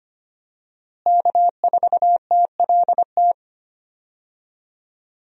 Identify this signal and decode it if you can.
Morse code, heard as a keyed tone, about 25 wpm, 700 Hz. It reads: K4TLT
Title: USA Call Signs - 5 Characters - Repeated 3x - 25wpm